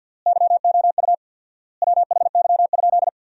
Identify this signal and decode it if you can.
Morse code, heard as a keyed tone, about 46 wpm, 700 Hz. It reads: YOU WHY?
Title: 4x Word-Spacing - Sentences from Top 200 Words - 50wpm